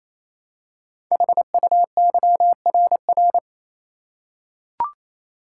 Morse code, heard as a keyed tone, about 27 wpm, 700 Hz. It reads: HUYRR
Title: Code Groups - 5 Letters - 28wpm